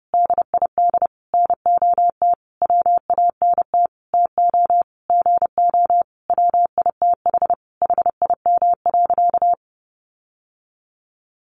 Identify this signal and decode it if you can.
Morse code, heard as a keyed tone, about 30 wpm, 700 Hz. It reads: DID NOT WANT TO GO WITH HIM.
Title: Sentences from Top 1300 Words 30wpm